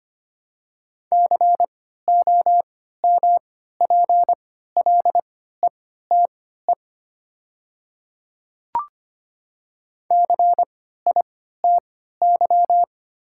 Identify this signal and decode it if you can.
Morse code, heard as a keyed tone, about 25 wpm, spaced as for 14 wpm, 700 Hz. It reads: COMPLETE CITY
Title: 3x Character Spacing - Top 500 Words - 25wpm